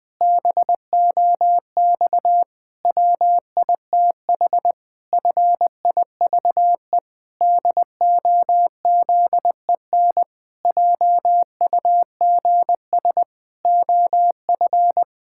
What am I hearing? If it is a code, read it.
Morse code, heard as a keyed tone, about 20 wpm, 700 Hz. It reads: BOX WITH FIVE DOZEN JUGS OF
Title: Pangram Challenge v2 - 20wpm